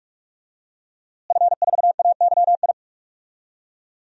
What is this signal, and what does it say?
Morse code, heard as a keyed tone, about 43 wpm, 700 Hz. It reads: F4AYI
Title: DXCC-100 Call Signs - 5 Characters - Repeated 2x - 45wpm